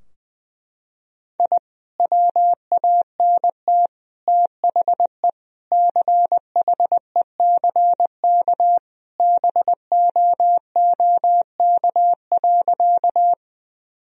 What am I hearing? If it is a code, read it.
Morse code, heard as a keyed tone, about 21 wpm, 700 Hz. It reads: I WANT THE CHECK BOOK.
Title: Sentences from Top 1000 Words - Review Part 3 - 20wpm